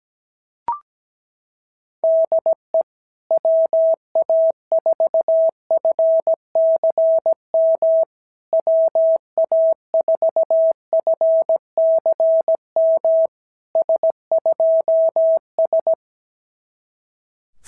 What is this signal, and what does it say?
Morse code, heard as a keyed tone, about 17 wpm, 650 Hz. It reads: DE WA4FCM WA4FCM S2S
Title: SOTA QSO Practice 17wpm